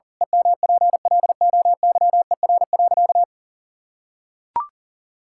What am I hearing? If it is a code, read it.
Morse code, heard as a keyed tone, about 40 wpm, 700 Hz. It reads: EMPLOYER.
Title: Pangram Challenge v2 - 40wpm